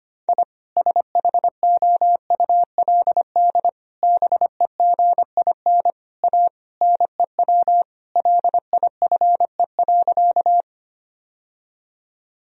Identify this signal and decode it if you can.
Morse code, heard as a keyed tone, about 25 wpm, 700 Hz. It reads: I SHOULD BEGIN A NEW LIFE.
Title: Sentences from Top 1000 Words - Review Part 1 - 25wpm